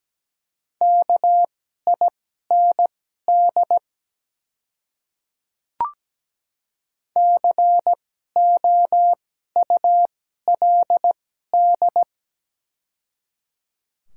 Morse code, heard as a keyed tone, about 17 wpm, spaced as for 12 wpm, 700 Hz. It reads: KIND COULD
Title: 2x Character Spacing - Top 200 Words - 17wpm